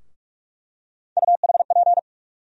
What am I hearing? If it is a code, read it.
Morse code, heard as a keyed tone, about 42 wpm, 700 Hz. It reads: UHP
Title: USA Call Sign Postfixes - 3 Letters 45wpm